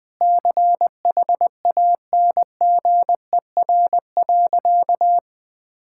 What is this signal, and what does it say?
Morse code, heard as a keyed tone, about 19 wpm, 700 Hz. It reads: CHANGER.
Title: Sentences with 6-Word Phrases - 20wpm